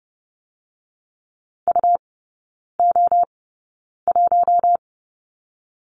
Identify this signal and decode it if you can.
Morse code, heard as a keyed tone, about 30 wpm, spaced as for 9 wpm, 700 Hz. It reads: UO1
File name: Lesson 18e - Letter m - Three Characters 30wpm